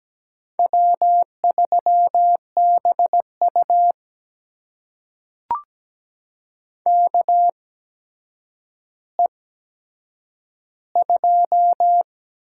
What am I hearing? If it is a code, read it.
Morse code, heard as a keyed tone, about 17 wpm, spaced as for 8 wpm, 700 Hz. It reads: W3BU KE2
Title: USA Call Signs - 8x Character Spacing - 17wpm